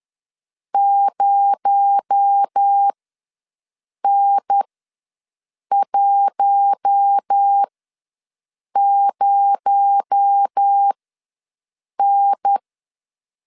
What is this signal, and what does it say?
Morse code, heard as a keyed tone, about 10 wpm, spaced as for 6 wpm, 800 Hz. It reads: 0N10N
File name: LEZIONI / 6 PKX0 / 6 PKX0 ws 10wpm 4220